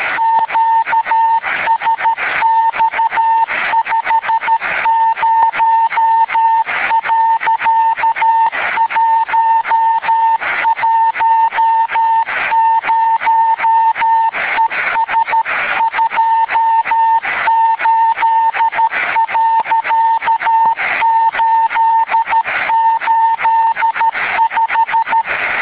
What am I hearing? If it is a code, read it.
Morse code, heard as a keyed tone, about 13 wpm, 900 Hz. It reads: QSX 50.110 ES 28.885